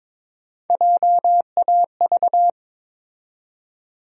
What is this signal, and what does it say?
Morse code, heard as a keyed tone, about 21 wpm, 700 Hz. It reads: JAV
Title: USA Call Sign Postfixes - 3 Letters 22wpm